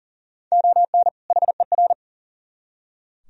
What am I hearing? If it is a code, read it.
Morse code, heard as a keyed tone, about 51 wpm, 700 Hz. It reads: ON HER